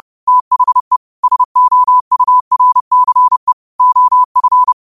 Morse code, heard as a keyed tone, about 30 wpm, 1000 Hz. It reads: THE SOURCE OF